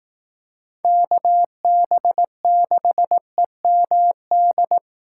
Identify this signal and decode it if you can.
Morse code, heard as a keyed tone, about 18 wpm, 700 Hz. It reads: KB6EMD
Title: Speed Racing - USA 2x3 Call Signs 15wpm